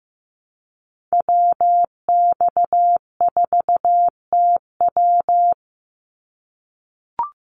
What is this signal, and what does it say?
Morse code, heard as a keyed tone, about 15 wpm, 700 Hz. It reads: WX4TW
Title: NA SOTA Activator Callsigns 15wpm